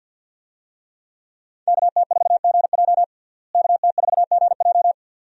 Morse code, heard as a keyed tone, about 52 wpm, 700 Hz. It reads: KT4GJ KT4GJ
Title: USA Call Signs - 5 Characters - Repeated 2x - 50wpm